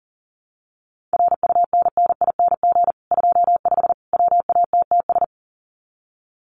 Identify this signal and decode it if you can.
Morse code, heard as a keyed tone, about 40 wpm, 700 Hz. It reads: RUNNING 25 WATTS
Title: Common QSO Phrases 40wpm